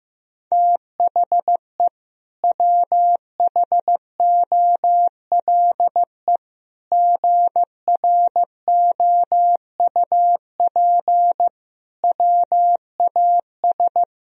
Morse code, heard as a keyed tone, about 15 wpm, 700 Hz. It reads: THE WHOLE GROUP WAS
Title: Sentences from Top 1000 Words - Review Part 1 - 15wpm